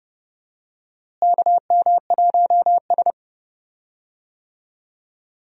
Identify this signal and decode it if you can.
Morse code, heard as a keyed tone, about 30 wpm, 700 Hz. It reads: KM1S
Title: USA Call Signs - 4 Characters - Repeated 3x - 30wpm